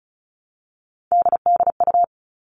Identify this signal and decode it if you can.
Morse code, heard as a keyed tone, about 34 wpm, 700 Hz. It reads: DDU
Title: All Cut Numbers - 3-Digit 35wpm